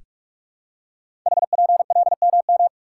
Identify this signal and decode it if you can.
Morse code, heard as a keyed tone, about 44 wpm, 690 Hz. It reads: HPLMM
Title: Code Groups - 5 Letters - 45wpm